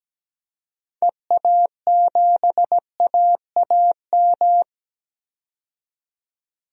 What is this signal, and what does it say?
Morse code, heard as a keyed tone, about 17 wpm, 700 Hz. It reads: EA7AAM
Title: DXCC-100 Call Signs - 6 Characters - 17wpm